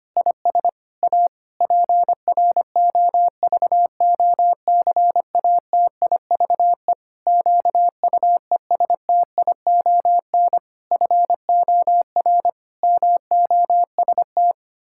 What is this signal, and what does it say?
Morse code, heard as a keyed tone, about 25 wpm, 700 Hz. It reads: IS A PROVOCATIVE QUESTION FOR MOST
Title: Pangram Challenge v2 - 25wpm